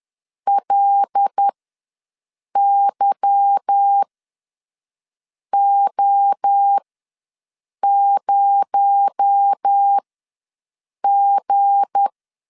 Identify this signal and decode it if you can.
Morse code, heard as a keyed tone, about 10 wpm, spaced as for 5 wpm, 800 Hz. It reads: LYO0G
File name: LEZIONI / 10 SIGNS / 10B INT ws 10wpm 4134